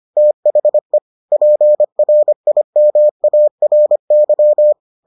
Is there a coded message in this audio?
Yes: Morse code from 0.2 to 4.7 s, heard as a keyed tone, about 25 wpm, 600 Hz. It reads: THE PRIMARY